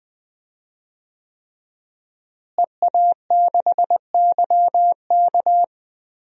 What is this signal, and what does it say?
Morse code, heard as a keyed tone, about 20 wpm, 700 Hz. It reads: EA6YK
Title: DXCC-100 Call Signs - 5 Characters - Repeated 3x - 20wpm